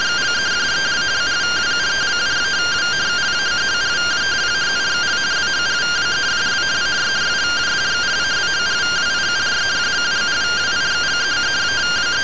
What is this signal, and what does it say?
Two-tone teleprinter signal, about 170 Hz shift.
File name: findings / MR AXIAL 71mm 3mm.gen-01 rtty rx